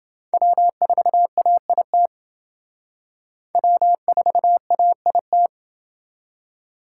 Morse code, heard as a keyed tone, about 28 wpm, 700 Hz. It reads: W4AIT W4AIT